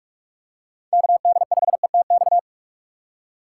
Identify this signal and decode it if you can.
Morse code, heard as a keyed tone, about 44 wpm, 690 Hz. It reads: KD5ETX
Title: USA Call Signs - 6 Characters - 3x Character Spacing - 45wpm